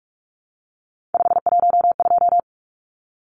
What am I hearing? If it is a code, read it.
Morse code, heard as a keyed tone, about 44 wpm, 700 Hz. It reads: 512
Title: Numbers - 3-Digit 45wpm